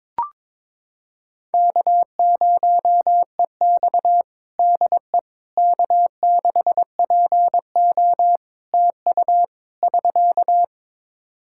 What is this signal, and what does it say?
Morse code, heard as a keyed tone, about 22 wpm, 700 Hz. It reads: K0EX DE K6PO TU <SK>
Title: Practice QSO - Exchange 4 - Ending - 22wpm